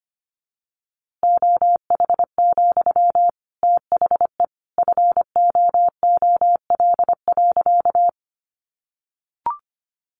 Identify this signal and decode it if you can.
Morse code, heard as a keyed tone, about 25 wpm, 700 Hz. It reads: OH, THE FOOL.
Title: A Sound Of Thunder 25wpm